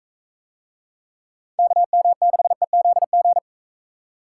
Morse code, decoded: KM6EZG

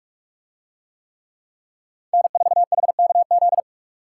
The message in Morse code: N4HKZ

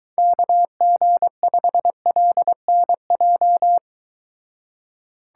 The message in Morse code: KG5LNJ

